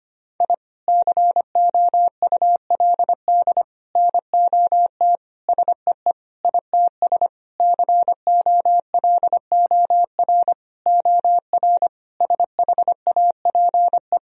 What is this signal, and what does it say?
Morse code, heard as a keyed tone, about 25 wpm, 700 Hz. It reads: I COULD NOT SEE ITS COLOR OR SHAPE
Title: Sentences from Top 1000 Words - Review Part 3 - 25wpm